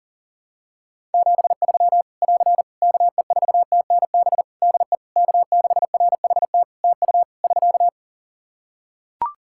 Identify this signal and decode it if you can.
Morse code, heard as a keyed tone, about 40 wpm, 700 Hz. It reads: 73 <AR> KE4TNB DE K6RHT TU <SK>
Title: Practice QSO - Exchange 4 - Ending - 40wpm